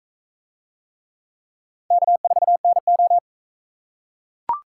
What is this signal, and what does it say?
Morse code, heard as a keyed tone, about 38 wpm, 700 Hz. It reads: K4NO